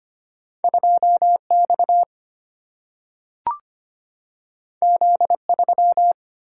Morse code, heard as a keyed tone, about 25 wpm, 700 Hz. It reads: 2X Z3
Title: Single-Letter Single-Number 25wpm